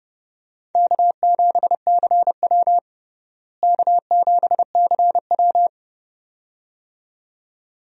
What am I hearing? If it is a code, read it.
Morse code, heard as a keyed tone, about 31 wpm, 700 Hz. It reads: K7CW K7CW
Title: USA Call Signs - 4 Characters - Repeated 2x - 30wpm